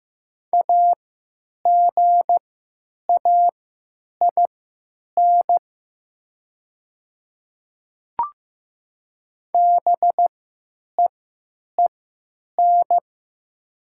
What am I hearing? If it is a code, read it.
Morse code, heard as a keyed tone, about 15 wpm, spaced as for 9 wpm, 700 Hz. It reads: AGAIN BEEN
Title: 3x Character Spacing - Top 200 Words - 15wpm